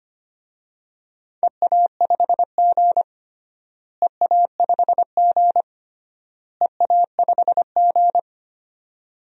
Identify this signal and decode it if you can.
Morse code, heard as a keyed tone, about 25 wpm, 700 Hz. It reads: EA5G EA5G EA5G
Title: DXCC-100 Call Signs - 4 Characters - Repeated 3x - 25wpm